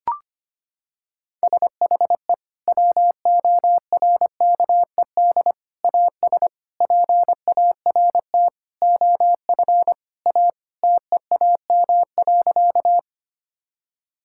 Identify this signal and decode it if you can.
Morse code, heard as a keyed tone, about 25 wpm, 700 Hz. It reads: SHE WORKED AS PART OF A TEAM.
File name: practice / 25wpm / Sentences with 4-Word Phrases - 25wpm